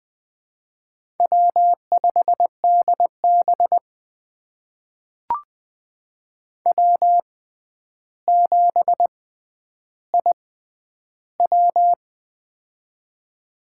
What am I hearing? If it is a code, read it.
Morse code, heard as a keyed tone, about 20 wpm, spaced as for 10 wpm, 700 Hz. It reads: W5DB W7IW
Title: USA Call Signs - 4 Characters - 6x Character Spacing - 20wpm